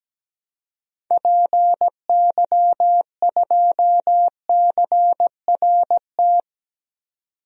Morse code, heard as a keyed tone, about 17 wpm, 700 Hz. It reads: PY2CRT
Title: DXCC-100 Call Signs - 6 Characters - 17wpm